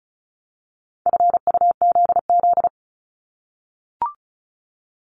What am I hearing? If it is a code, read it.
Morse code, heard as a keyed tone, about 36 wpm, 700 Hz. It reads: FUZZ
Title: Multi-letter Phonemes 35wpm